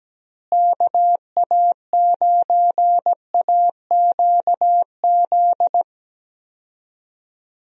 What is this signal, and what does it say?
Morse code, heard as a keyed tone, about 17 wpm, 700 Hz. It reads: KA9AQZ
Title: USA Call Signs - 6 Characters - Repeated 3x - 17wpm